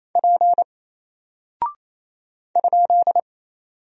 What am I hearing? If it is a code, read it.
Morse code, heard as a keyed tone, about 28 wpm, 700 Hz. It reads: P ?
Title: Single Character (Letter:Number:Punctuation:Prosign) Alt 28wpm